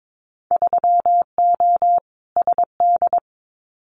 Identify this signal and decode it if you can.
Morse code, heard as a keyed tone, about 22 wpm, 700 Hz. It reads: 3O SD
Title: Winter Field Day Exchanges 22wpm